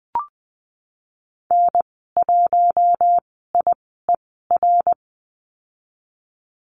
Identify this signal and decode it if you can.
Morse code, heard as a keyed tone, about 20 wpm, spaced as for 14 wpm, 700 Hz. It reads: N1IER
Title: 055 Lesson 10g - Letter r - Call Signs 20wpm